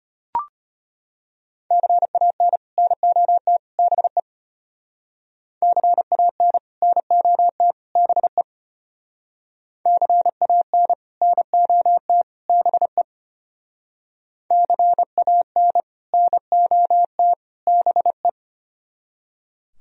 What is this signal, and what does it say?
Morse code, heard as a keyed tone, about 32 wpm, 700 Hz. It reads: CAN NOT BE CAN NOT BE CAN NOT BE CAN NOT BE